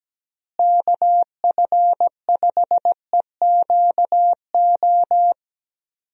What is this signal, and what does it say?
Morse code, heard as a keyed tone, about 17 wpm, 700 Hz. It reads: KF5EQO
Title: USA Call Signs - 6 Characters - Repeated 2x - 17wpm